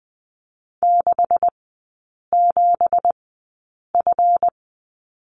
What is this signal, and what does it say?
Morse code, heard as a keyed tone, about 20 wpm, spaced as for 8 wpm, 700 Hz. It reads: 67F
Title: 166 Lesson 27e - Number 7 - Three Characters 20wpm